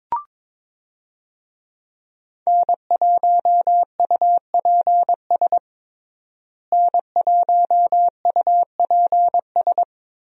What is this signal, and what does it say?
Morse code, heard as a keyed tone, about 22 wpm, 700 Hz. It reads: N1UPS N1UPS
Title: USA Call Signs - 5 Characters - Repeated 3x - 22wpm